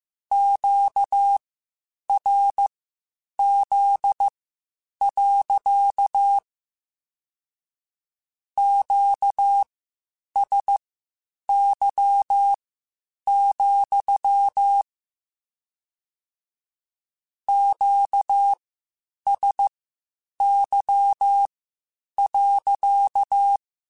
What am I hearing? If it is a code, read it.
Morse code, heard as a keyed tone, about 15 wpm, spaced as for 8 wpm, 800 Hz. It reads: QRZ. QSY, QSY.
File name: Q Calls / Q Calls-mp3s / Q Calls sppc @13